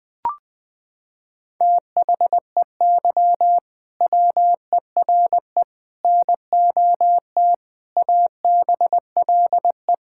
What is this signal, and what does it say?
Morse code, heard as a keyed tone, about 20 wpm, 700 Hz. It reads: THEY WERE NOT ABLE
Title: Sentences from Top 1000 Words - Review Part 6 - 20wpm